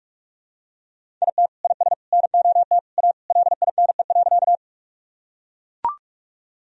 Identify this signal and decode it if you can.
Morse code, heard as a keyed tone, about 41 wpm, 700 Hz. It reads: IT IS NOT A LINE.
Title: Sentences from Top 1000 Words - Review Part 1 - 45wpm